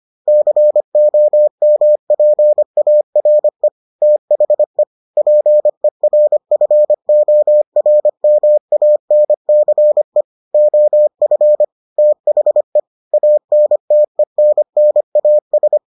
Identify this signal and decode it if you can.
Morse code, heard as a keyed tone, about 25 wpm, 600 Hz. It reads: COMPARE THE PERFORMANCE OF THE ANTENNAS